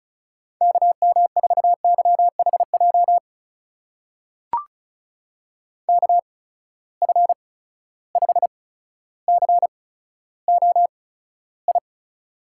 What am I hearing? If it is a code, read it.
Morse code, heard as a keyed tone, about 35 wpm, spaced as for 15 wpm, 700 Hz. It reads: KM4YHJ KF5COI
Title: USA Call Signs - 6 Characters - 8x Character Spacing - 35wpm